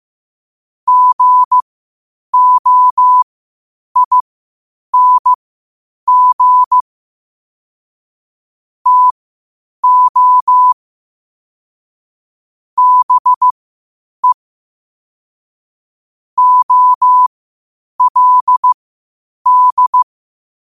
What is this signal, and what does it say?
Morse code, heard as a keyed tone, about 15 wpm, spaced as for 8 wpm, 1000 Hz. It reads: GOING TO BE OLD
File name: qotdfiles / 8wpm